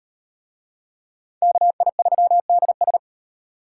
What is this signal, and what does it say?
Morse code, heard as a keyed tone, about 35 wpm, 700 Hz. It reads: KI3DS